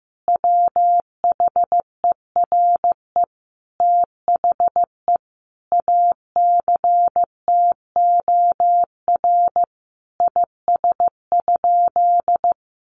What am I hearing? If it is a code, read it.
Morse code, heard as a keyed tone, about 15 wpm, 700 Hz. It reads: WHERE THE ACTOR IS?
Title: Sentences from Top 1500 Words 15wpm